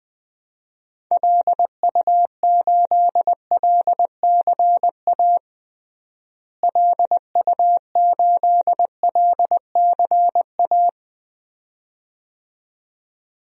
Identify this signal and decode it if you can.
Morse code, heard as a keyed tone, about 20 wpm, 700 Hz. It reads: LU8LCA LU8LCA
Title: DXCC-100 Call Signs - 6 Characters - Repeated 2x - 20wpm